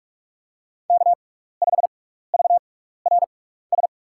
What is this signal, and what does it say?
Morse code, heard as a keyed tone, about 44 wpm, spaced as for 15 wpm, 700 Hz. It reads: K5VRS
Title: USA Call Signs - 6x Character Spacing - 45wpm